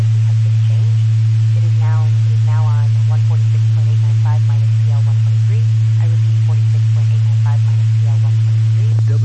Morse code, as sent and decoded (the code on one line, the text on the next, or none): none